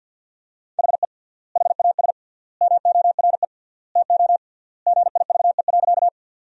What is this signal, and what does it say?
Morse code, heard as a keyed tone, about 50 wpm, 700 Hz. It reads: HE HAS MORE TO GIVE.
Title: 3x Word-Spacing - Sentences from Top 200 Words - 50wpm